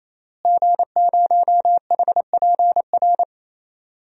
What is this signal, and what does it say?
Morse code, heard as a keyed tone, about 27 wpm, 700 Hz. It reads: G0HPR
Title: DXCC-100 Call Signs - 5 Characters - Repeated 3x - 28wpm